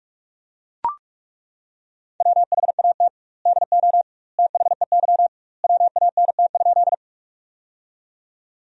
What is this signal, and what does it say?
Morse code, heard as a keyed tone, about 42 wpm, 700 Hz. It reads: WHAT DO THEY WANT?
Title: 2x Word-Spacing - Sentences from Top 200 Words - 45wpm